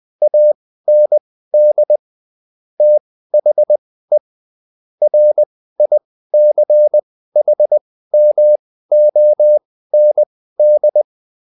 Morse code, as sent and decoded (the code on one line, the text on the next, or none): .- -. -.. / - .... / .-. .. -.-. .... -- --- -. -..
AND THE RICHMOND